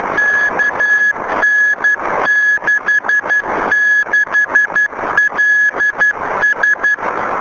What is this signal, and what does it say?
Morse code, heard as a keyed tone, about 12 wpm, 1700 Hz, with noise. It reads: KN66LS
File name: UT5G B